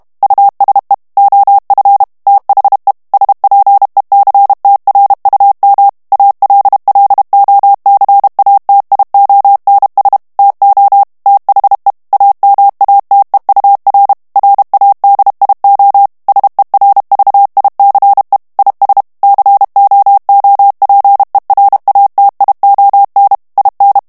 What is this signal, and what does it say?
Morse code, heard as a keyed tone, about 32 wpm, 800 Hz. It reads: USE OF THE SPECTRUM ALLOCATIONS TO THE AMATEUR RADIO SERVICE IS COOPERATION IN
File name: morse mp3 / ANGLAIS texte clair 3 vitesse 30 mots minute